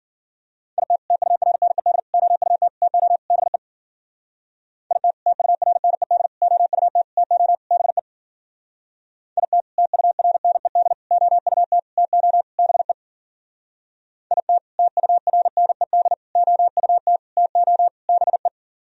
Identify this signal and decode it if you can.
Morse code, heard as a keyed tone, about 51 wpm, 700 Hz. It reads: IT TURNED OUT TO BE IT TURNED OUT TO BE IT TURNED OUT TO BE IT TURNED OUT TO BE